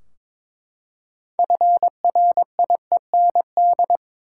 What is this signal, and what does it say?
Morse code, heard as a keyed tone, about 22 wpm, 700 Hz. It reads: FRIEND
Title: Six Letter Words - 22wpm